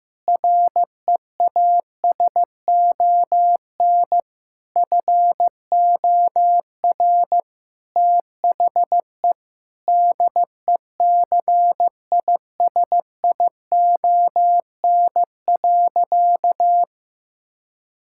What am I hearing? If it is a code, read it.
Morse code, heard as a keyed tone, about 15 wpm, 700 Hz. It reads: REASON FOR THE DECISION.
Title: Sentences from Top 1600 Words 15wpm